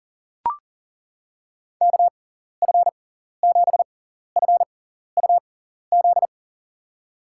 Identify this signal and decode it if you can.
Morse code, heard as a keyed tone, about 40 wpm, spaced as for 14 wpm, 700 Hz. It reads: KF7FUZ
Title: USA Call Signs - 6 Characters - 6x Character Spacing - 40wpm